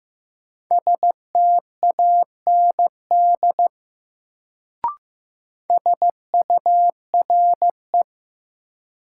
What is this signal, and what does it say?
Morse code, heard as a keyed tone, about 15 wpm, 700 Hz. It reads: STAND SURE
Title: Sentences from Top 300 Words - New Word Drill - 15wpm